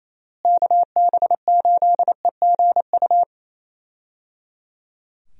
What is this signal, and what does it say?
Morse code, heard as a keyed tone, about 28 wpm, 700 Hz. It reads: KB8EGU
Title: USA Call Signs - 6 Characters - Repeated 3x - 28wpm